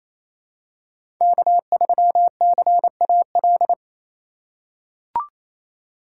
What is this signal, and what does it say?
Morse code, heard as a keyed tone, about 28 wpm, 700 Hz. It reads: K3CAL
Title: USA Call Signs - 5 Characters - 6x Character Spacing - 28wpm